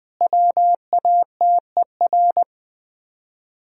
Morse code, heard as a keyed tone, about 20 wpm, 700 Hz. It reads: WATER
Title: Top 100 Words Repeated 20wpm